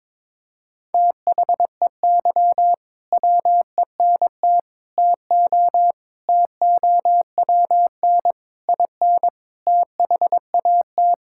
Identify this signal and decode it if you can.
Morse code, heard as a keyed tone, about 22 wpm, 700 Hz. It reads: THEY WENT TO TOWN IN THAT
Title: Sentences from Top 1000 Words - Review Part 5 - 22wpm